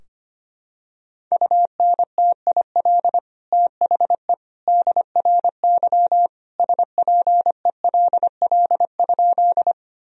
Morse code, heard as a keyed tone, about 25 wpm, 700 Hz. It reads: UNTIL THE DRY SPELL?
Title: Sentences from Top 1000 Words - Review Part 3 - 25wpm